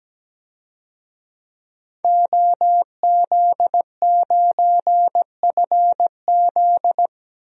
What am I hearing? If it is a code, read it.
Morse code, heard as a keyed tone, about 17 wpm, 700 Hz. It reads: OZ9FZ
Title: DXCC-100 Call Signs - 5 Characters - Repeated 3x - 17wpm